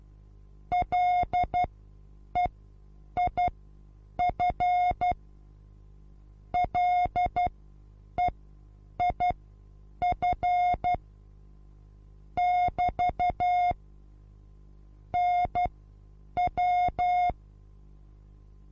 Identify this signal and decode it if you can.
Morse code, heard as a keyed tone, about 12 wpm, spaced as for 8 wpm, 720 Hz. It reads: LEIF LEIF <BT> NW
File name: repos / Morsekurs / DJ4UF Morsen23